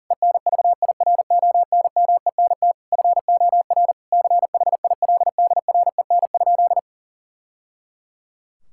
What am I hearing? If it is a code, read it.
Morse code, heard as a keyed tone, about 40 wpm, 700 Hz. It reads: ENVIRONMENT FOR CHILDREN?
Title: Sentences from Top 1000 Words - Review Part 5 - 40wpm